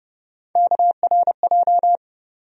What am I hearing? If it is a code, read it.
Morse code, heard as a keyed tone, about 30 wpm, 700 Hz. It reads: KRJ
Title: USA Call Sign Postfixes - 3 Letters 30wpm